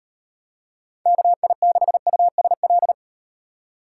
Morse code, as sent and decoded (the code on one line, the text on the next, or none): -.- .. -.... ..- ... .-..
KI6USL